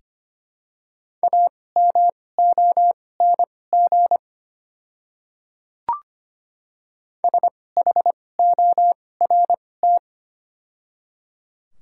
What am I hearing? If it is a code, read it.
Morse code, heard as a keyed tone, about 25 wpm, spaced as for 18 wpm, 700 Hz. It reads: AMONG SHORT
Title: 2x Character Spacing - Top 500 Words - 25wpm